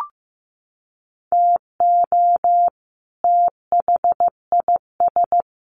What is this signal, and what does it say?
Morse code, heard as a keyed tone, about 15 wpm, 700 Hz. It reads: TO THIS
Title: Sets of 2 Words 15wpm